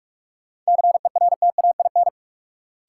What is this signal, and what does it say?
Morse code, heard as a keyed tone, about 42 wpm, 700 Hz. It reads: CERTAIN
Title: Sentences from Top 500 Words - New Word Drill - 45wpm